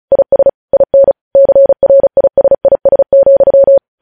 Morse code, heard as a keyed tone, about 36 wpm, 550 Hz. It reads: IS IN CRISIS,
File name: podcast / News headlines-1x-35x35